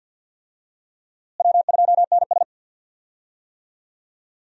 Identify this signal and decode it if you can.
Morse code, heard as a keyed tone, about 52 wpm, 700 Hz. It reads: W2NS
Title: USA Call Signs - 4 Characters - Repeated 2x - 50wpm